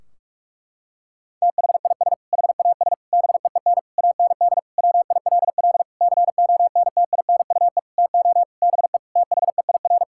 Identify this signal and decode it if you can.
Morse code, heard as a keyed tone, about 41 wpm, 700 Hz. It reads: THIS HAS BEEN AND WILL CONTINUE TO BE THEIR
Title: Sentences from Top 1600 Words 45wpm